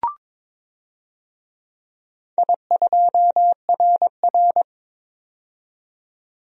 Morse code, heard as a keyed tone, about 22 wpm, 700 Hz. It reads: I2RR
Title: DXCC-100 Call Signs - 4 Characters - 22wpm